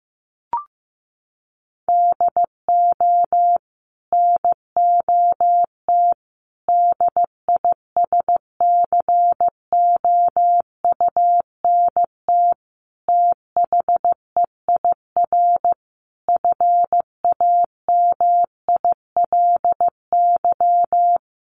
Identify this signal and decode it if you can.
Morse code, heard as a keyed tone, about 15 wpm, 700 Hz. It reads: DO NOT DISCOUNT THEIR FAMILY
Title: Sentences from Top 1200 Words 15wpm